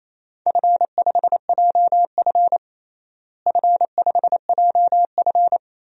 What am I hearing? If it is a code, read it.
Morse code, heard as a keyed tone, about 28 wpm, 700 Hz. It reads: F5JF F5JF
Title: DXCC-100 Call Signs - 4 Characters - Repeated 3x - 28wpm